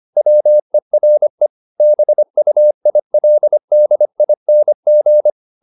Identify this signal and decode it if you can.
Morse code, heard as a keyed tone, about 25 wpm, 600 Hz. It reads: WERE BUILDING
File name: the old man and the sea part 1 25wpm